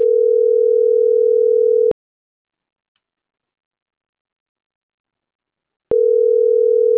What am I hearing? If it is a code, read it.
Ringback tone (North America), 440 + 480 Hz.